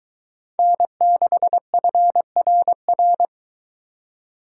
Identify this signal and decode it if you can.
Morse code, heard as a keyed tone, about 22 wpm, 700 Hz. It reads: N6FRR